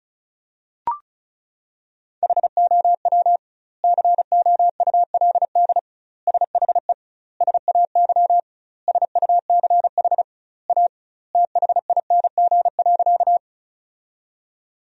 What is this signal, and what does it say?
Morse code, heard as a keyed tone, about 34 wpm, 700 Hz. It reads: HOW COULD SHE SAY SUCH A THING.
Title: 2x Word-Spacing - Sentences from Top 200 Words - 35wpm